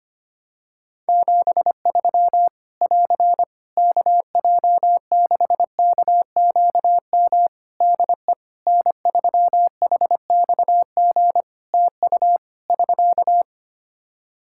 Morse code, heard as a keyed tone, about 25 wpm, 700 Hz. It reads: 73 <AR> KJ6KQM DE N3HXG TU <SK>
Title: Practice QSO - Exchange 4 - Ending - 25wpm